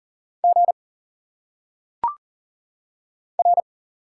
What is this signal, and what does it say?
Morse code, heard as a keyed tone, about 40 wpm, 700 Hz. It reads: G R
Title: Single Letters - Phonetic Pronunciation 40wpm